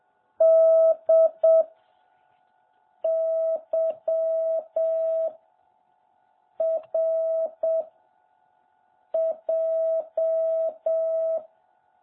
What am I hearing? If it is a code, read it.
Morse code, heard as a keyed tone, about 7 wpm, 660 Hz. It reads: D Y R J
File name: jh8cbh / 2021cw